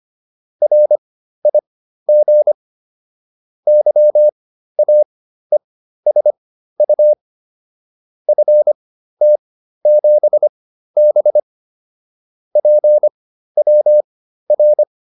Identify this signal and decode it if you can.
Morse code, heard as a keyed tone, about 25 wpm, spaced as for 13 wpm, 600 Hz. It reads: RIG YAESU FT7B PWR
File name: QSO 407 13